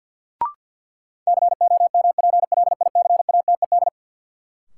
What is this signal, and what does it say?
Morse code, heard as a keyed tone, about 52 wpm, 700 Hz. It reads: COMPLICATED
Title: Sentences from Top 1200 Words - New Word Drill - 50wpm